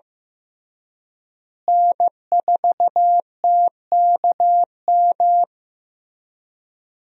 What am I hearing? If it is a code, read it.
Morse code, heard as a keyed tone, about 15 wpm, 700 Hz. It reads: N4TKM
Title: USA Call Signs - 5 Characters - Repeated 3x - 15wpm